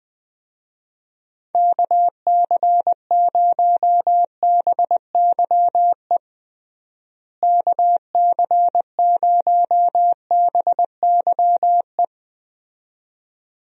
Morse code, heard as a keyed tone, about 20 wpm, 700 Hz. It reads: KC0BYE KC0BYE
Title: USA Call Signs - 6 Characters - Repeated 2x - 20wpm